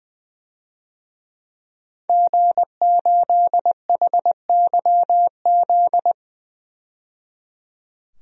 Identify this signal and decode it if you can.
Morse code, heard as a keyed tone, about 20 wpm, 700 Hz. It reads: G8HYZ
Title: DXCC-100 Call Signs - 5 Characters - 20wpm